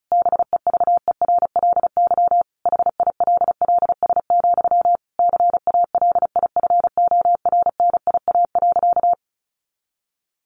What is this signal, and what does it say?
Morse code, heard as a keyed tone, about 34 wpm, 700 Hz. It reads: BEVERLY HILLS, CALIFORNIA.
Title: Easy Phrases 35wpm